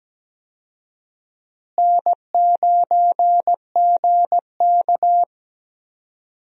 Morse code, heard as a keyed tone, about 17 wpm, 700 Hz. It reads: N9GK
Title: USA Call Signs - 4 Characters - Repeated 3x - 17wpm